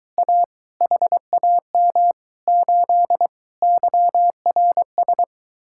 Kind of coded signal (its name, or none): Morse code